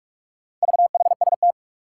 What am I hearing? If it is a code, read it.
Morse code, heard as a keyed tone, about 44 wpm, 700 Hz. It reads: VHST
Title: Four Challenging Letter-Number Combinations 45wpm